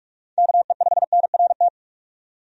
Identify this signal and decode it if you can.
Morse code, heard as a keyed tone, about 46 wpm, 700 Hz. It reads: KE5NRT
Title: USA Call Signs - 6 Characters - 45wpm